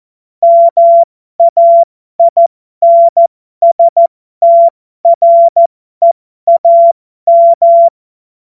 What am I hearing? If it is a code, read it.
Morse code, heard as a keyed tone, about 14 wpm, spaced as for 12 wpm, 680 Hz. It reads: MAINSTREAM
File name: Morsekurs / Witze